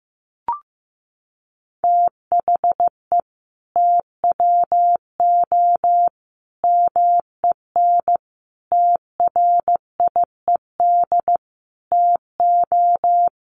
Morse code, heard as a keyed tone, about 15 wpm, 700 Hz. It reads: THE TWO MEN TRIED TO
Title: A Sound Of Thunder 15wpm